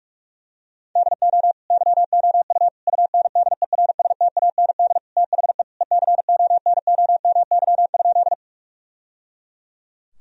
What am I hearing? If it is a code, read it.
Morse code, heard as a keyed tone, about 42 wpm, 700 Hz. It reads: DO YOU UNDERSTAND THE ECONOMY?
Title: Sentences from Top 1000 Words - Review Part 3 - 45wpm